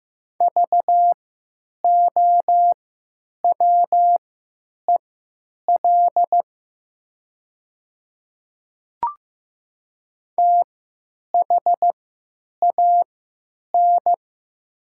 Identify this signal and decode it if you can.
Morse code, heard as a keyed tone, about 15 wpm, spaced as for 9 wpm, 700 Hz. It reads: VOWEL THAN
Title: 3x Character Spacing - Top 500 Words - 15wpm